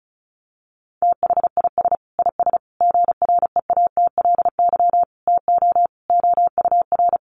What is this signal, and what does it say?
Morse code, heard as a keyed tone, about 34 wpm, 700 Hz. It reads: THIS IS GREATLY TO OUR
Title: Sentences from Top 1400 Words 35wpm